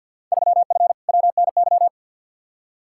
Morse code, heard as a keyed tone, about 53 wpm, 700 Hz. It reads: 3F WNY
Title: Field Day Exchanges - 50wpm